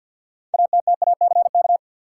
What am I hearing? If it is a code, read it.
Morse code, heard as a keyed tone, about 52 wpm, 700 Hz. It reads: ATTACK
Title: Sentences from Top 1500 Words - New Word Drill Alt - 50wpm